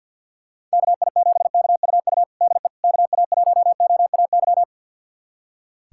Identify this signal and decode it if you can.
Morse code, heard as a keyed tone, about 47 wpm, 700 Hz. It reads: KI7KUU DE KA1OAY
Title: USA Callsign DE Callsign 50wpm